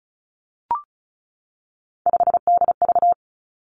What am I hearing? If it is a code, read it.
Morse code, heard as a keyed tone, about 36 wpm, 700 Hz. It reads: 5DV
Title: Three Challenging Letter-Number Combinations 35wpm